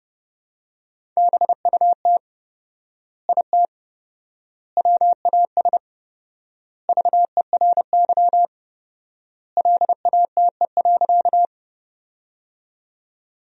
Morse code, heard as a keyed tone, about 32 wpm, 700 Hz. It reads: BUT IT WAS VERY LATE.
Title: 4x Word-Spacing - Sentences from Top 300 Words - 30wpm